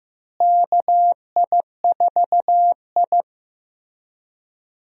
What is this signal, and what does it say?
Morse code, heard as a keyed tone, about 15 wpm, 700 Hz. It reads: KI4I
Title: USA Call Signs - 4 Characters - Repeated 2x - 15wpm